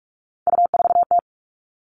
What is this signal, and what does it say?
Morse code, heard as a keyed tone, about 44 wpm, 700 Hz. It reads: U4T
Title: All Cut Numbers - 3-Digit 45wpm